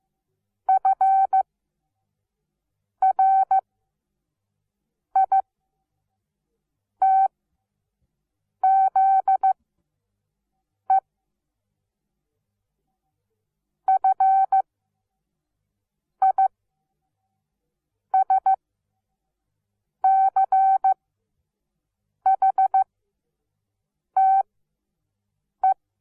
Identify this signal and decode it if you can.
Morse code, heard as a keyed tone, about 15 wpm, spaced as for 5 wpm, 770 Hz. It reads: FRITZE FISCHTE
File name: Onlinelehrgang / morsen / DJ4UF Morsen17